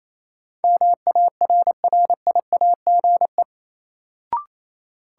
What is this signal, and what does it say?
Morse code, heard as a keyed tone, about 28 wpm, 700 Hz. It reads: MARRIAGE
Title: Sentences from Top 800 Words - New Word Drill - 28wpm